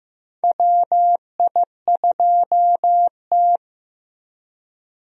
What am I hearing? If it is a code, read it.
Morse code, heard as a keyed tone, about 15 wpm, 700 Hz. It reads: WI2T
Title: USA Call Signs - 4 Characters - Repeated 2x - 15wpm